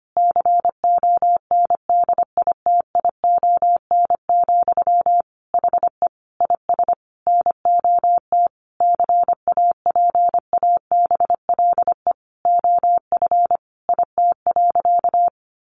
Morse code, heard as a keyed tone, about 25 wpm, 700 Hz. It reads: CONDITION, HE IS NOT CAPABLE OF IT.
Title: Sentences from Top 1200 Words 25wpm